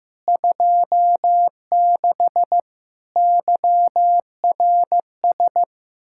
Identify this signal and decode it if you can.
Morse code, heard as a keyed tone, about 15 wpm, 700 Hz. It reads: 26 YRS
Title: Practice QSO - Exchange 3 - More Info - 15wpm